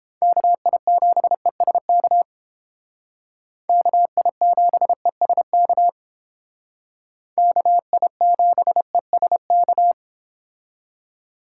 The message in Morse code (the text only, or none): KI7ESK KI7ESK KI7ESK